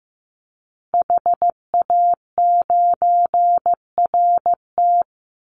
Morse code, heard as a keyed tone, about 15 wpm, 700 Hz. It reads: HA9RT
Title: Call Signs Repeated 2x - 15wpm